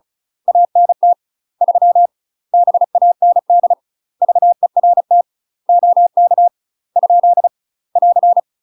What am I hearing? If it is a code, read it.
Morse code, heard as a keyed tone, about 35 wpm, 700 Hz. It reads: ANT 3 BAND VERT OK ? <AR>
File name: UK MASTER QSOs TRAINING / 35WPM / UK MASTER QSOs 0007